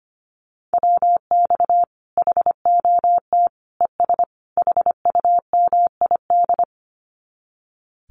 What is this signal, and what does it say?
Morse code, heard as a keyed tone, about 25 wpm, 700 Hz. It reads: WX HOT ES HUMID
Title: Common QSO Phrases 25wpm